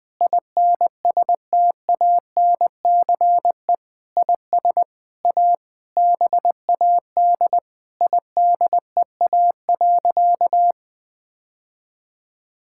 Morse code, heard as a keyed tone, about 20 wpm, 700 Hz. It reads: INSTANCE IS A BAD IDEA.
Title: Sentences from Top 1000 Words - Review Part 5 - 20wpm